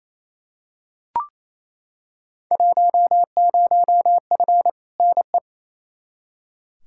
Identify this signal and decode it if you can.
Morse code, heard as a keyed tone, about 28 wpm, 700 Hz. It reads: 10F NE
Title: Field Day Exchanges Encore - 28wpm